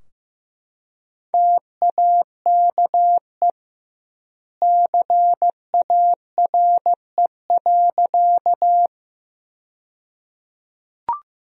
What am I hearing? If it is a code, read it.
Morse code, heard as a keyed tone, about 15 wpm, 700 Hz. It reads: TAKE CARE.
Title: 2x Word-Spacing - Sentences from Top 300 Words - 15wpm